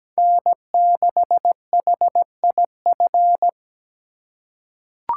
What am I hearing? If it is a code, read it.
Morse code, heard as a keyed tone, about 17 wpm, 700 Hz. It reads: N6HIF
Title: USA Call Signs - 5 Characters - 4x Character Spacing - 17wpm